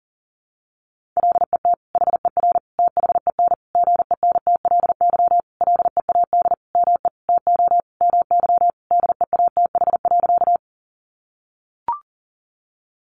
Morse code, heard as a keyed tone, about 40 wpm, 700 Hz. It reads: LET HER THEN GENTLY LEAD ME TO MY DEATH.
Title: Sentences from Top 1500 Words 40wpm